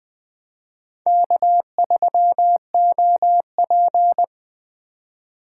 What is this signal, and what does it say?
Morse code, heard as a keyed tone, about 19 wpm, 700 Hz. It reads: K3OP